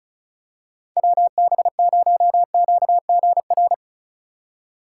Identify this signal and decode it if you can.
Morse code, heard as a keyed tone, about 36 wpm, 700 Hz. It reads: WB0QGR
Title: USA Call Signs - 6 Characters - 3x Character Spacing - 35wpm